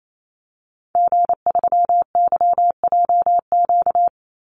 Morse code, heard as a keyed tone, about 28 wpm, 700 Hz. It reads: G3YJQ
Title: Call Signs - 28wpm